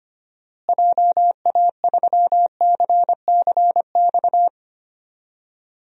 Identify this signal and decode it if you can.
Morse code, heard as a keyed tone, about 25 wpm, 700 Hz. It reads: JA3CCX
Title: DXCC-100 Call Signs - 6 Characters - Repeated 3x - 25wpm